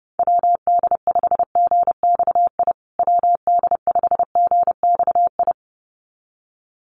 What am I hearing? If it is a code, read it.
Morse code, heard as a keyed tone, about 30 wpm, 700 Hz. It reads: WD5GXI WD5GXI
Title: USA Call Signs Repeated 2x 30wpm